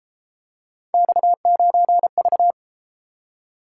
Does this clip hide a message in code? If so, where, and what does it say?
Yes: Morse code from 0.9 to 2.5 s, heard as a keyed tone, about 32 wpm, 700 Hz. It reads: X9V